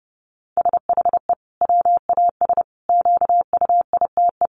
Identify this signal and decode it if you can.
Morse code, heard as a keyed tone, about 30 wpm, 700 Hz. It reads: SHE WAS QUITE
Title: Sentences from Top 700 Words 30wpm